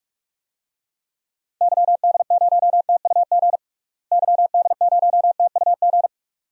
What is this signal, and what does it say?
Morse code, heard as a keyed tone, about 47 wpm, 700 Hz. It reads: YD0TUG YD0TUG
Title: DXCC-100 Call Signs - 6 Characters - Repeated 2x - 45wpm